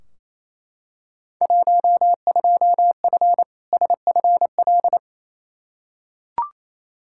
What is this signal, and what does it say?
Morse code, heard as a keyed tone, about 28 wpm, 700 Hz. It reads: 12F SFL
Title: Field Day Exchanges Encore - 28wpm